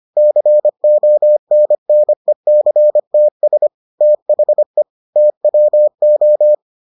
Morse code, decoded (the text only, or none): CONNECTS THE TWO